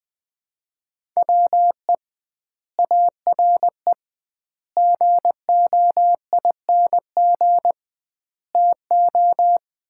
Morse code, heard as a keyed tone, about 20 wpm, 700 Hz. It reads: WE ARE GOING TO
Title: 2x Word-Spacing - Sets of 4 Words - 20wpm